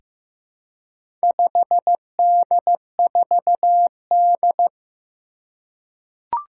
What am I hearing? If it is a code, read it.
Morse code, heard as a keyed tone, about 15 wpm, 700 Hz. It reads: 5D4D
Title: Four Challenging Letter-Number Combinations 15wpm